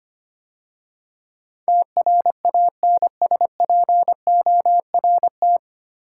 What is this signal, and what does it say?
Morse code, heard as a keyed tone, about 25 wpm, 700 Hz. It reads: TRANSPORT
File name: practice / 25wpm / 9 Letter Words - First 3 Letters Given - 25wpm